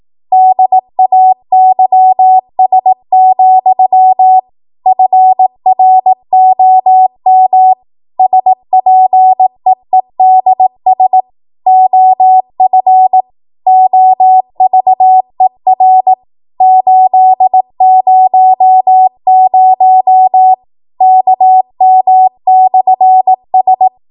Morse code, heard as a keyed tone, about 18 wpm, 750 Hz. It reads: DAYS, FROM SPEEDS OF OVER 800 KM/S
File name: Morse / 18WPM